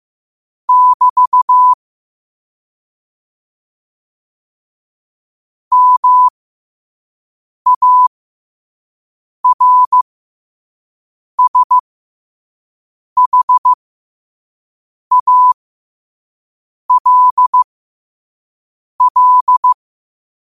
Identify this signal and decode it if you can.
Morse code, heard as a keyed tone, about 15 wpm, spaced as for 5 wpm, 1000 Hz. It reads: <BT> MARSHALL
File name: qotdfiles / 5wpm